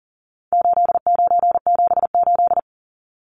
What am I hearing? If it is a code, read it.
Morse code, heard as a keyed tone, about 40 wpm, 700 Hz. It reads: 8978
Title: Numbers - 4-Digit 40wpm